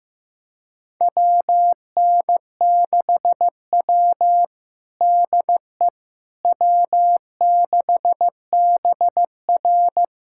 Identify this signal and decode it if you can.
Morse code, heard as a keyed tone, about 15 wpm, 700 Hz. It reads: WN6W DE W6BR
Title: USA Callsign DE Callsign - 4 Characters - 15wpm